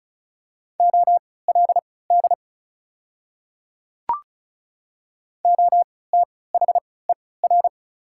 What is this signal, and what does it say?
Morse code, heard as a keyed tone, about 34 wpm, spaced as for 20 wpm, 700 Hz. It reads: OLD OTHER
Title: 3x Character Spacing - Top 200 Words - 35wpm